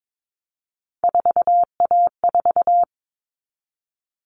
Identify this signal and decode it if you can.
Morse code, heard as a keyed tone, about 22 wpm, 700 Hz. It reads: 4A4
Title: Common Cut Numbers - 3-Digit 22wpm